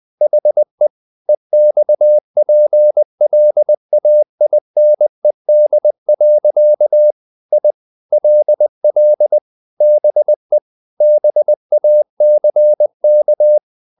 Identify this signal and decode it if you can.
Morse code, heard as a keyed tone, about 20 wpm, 600 Hz. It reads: HE EXPLAINED. I LL BE BACK
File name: the old man and the sea / the old man and the sea part 1 20wpm